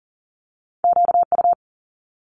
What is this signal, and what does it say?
Morse code, heard as a keyed tone, about 37 wpm, 700 Hz. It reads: QU
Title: Multi-letter Phonemes Quick Fire 40wpm